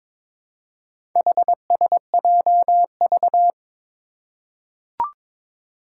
Morse code, heard as a keyed tone, about 22 wpm, 700 Hz. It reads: HSJV
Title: Four Challenging Letter-Number Combinations 22wpm